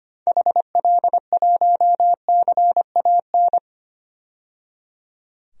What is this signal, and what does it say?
Morse code, heard as a keyed tone, about 25 wpm, 700 Hz. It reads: HL1CAN
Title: DXCC-100 Call Signs - 6 Characters - 25wpm